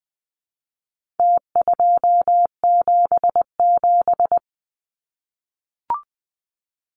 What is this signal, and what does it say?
Morse code, heard as a keyed tone, about 20 wpm, 700 Hz. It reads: T277
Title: Numbers - Contest Style - 4-Digit 20wpm